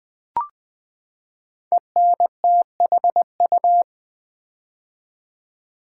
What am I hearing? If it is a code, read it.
Morse code, heard as a keyed tone, about 20 wpm, 700 Hz. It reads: ENTHU
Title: Build A Word - 5 Letters At A Time - 12-Letter Words - 20wpm